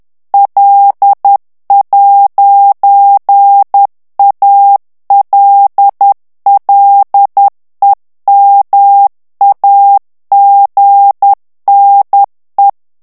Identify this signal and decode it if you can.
Morse code, heard as a keyed tone, about 10 wpm, 800 Hz. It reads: L'ALLEMAGNE
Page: Français - texte en clair 1 - vitesse : 10 mots minute